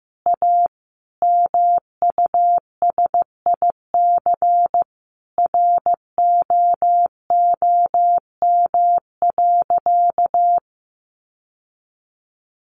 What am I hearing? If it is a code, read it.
Morse code, heard as a keyed tone, about 15 wpm, 700 Hz. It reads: A MUSIC ROOM.
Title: Sentences from Top 700 Words 15wpm